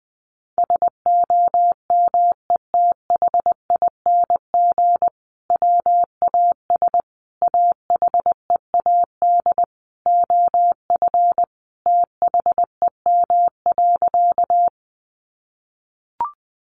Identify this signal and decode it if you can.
Morse code, heard as a keyed tone, about 20 wpm, 700 Hz. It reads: SOMETHING WAS AHEAD OF THEM.
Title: Sentences from Top 1100 Words 20wpm